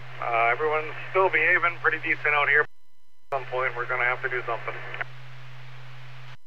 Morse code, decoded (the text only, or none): none